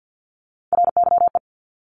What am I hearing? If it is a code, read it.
Morse code, heard as a keyed tone, about 50 wpm, 700 Hz. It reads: RYE
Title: Multi-letter Phonemes 50wpm